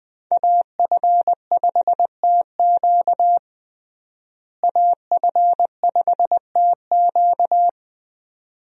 Morse code, heard as a keyed tone, about 20 wpm, 700 Hz. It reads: AF5TQ AF5TQ
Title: USA Call Signs - 5 Characters - Repeated 2x - 20wpm